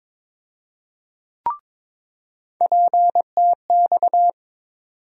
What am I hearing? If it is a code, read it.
Morse code, heard as a keyed tone, about 22 wpm, 700 Hz. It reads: PTX
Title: USA Call Sign Postfixes - 3 Letters 22wpm